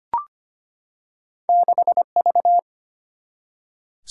Morse code, heard as a keyed tone, about 25 wpm, 700 Hz. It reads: 6V
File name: Two Letter-Number Combinations - 25wpm